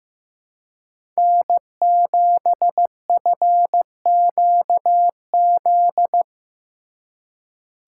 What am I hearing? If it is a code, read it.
Morse code, heard as a keyed tone, about 15 wpm, 700 Hz. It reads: N7FQZ
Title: USA Call Signs - 2x Character Spacing - 15wpm